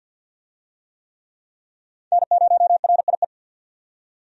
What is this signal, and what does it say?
Morse code, heard as a keyed tone, about 50 wpm, 700 Hz. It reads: N0RIE
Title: USA Call Signs - 5 Characters - 50wpm